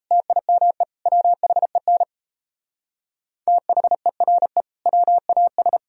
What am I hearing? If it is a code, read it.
Morse code, heard as a keyed tone, about 35 wpm, 700 Hz. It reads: TIME WHEN THERE WAS